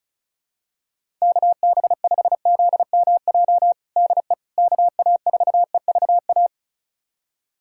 Morse code, heard as a keyed tone, about 36 wpm, 700 Hz. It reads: KB5ZMJ DE KA4EVA
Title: USA Callsign DE Callsign - 6 Characters - 35wpm